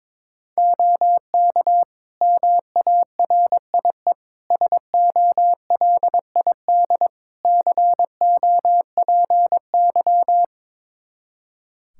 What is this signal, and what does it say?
Morse code, heard as a keyed tone, about 22 wpm, 700 Hz. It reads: OK MARIE SOLID COPY
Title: Practice QSO - Exchange 3 - More Info - 22wpm